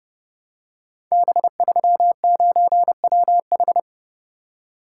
Morse code, heard as a keyed tone, about 30 wpm, 700 Hz. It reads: B39WH
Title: Code Groups - 5 Letter:Numbers - 30wpm